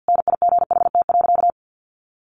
Morse code, heard as a keyed tone, about 50 wpm, 700 Hz. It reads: NIGHT.
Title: Sentences from Top 1300 Words 50wpm